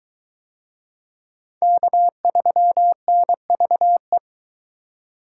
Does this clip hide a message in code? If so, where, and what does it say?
Yes: Morse code from 1.6 to 4.2 s, heard as a keyed tone, about 21 wpm, 700 Hz. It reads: K3NVE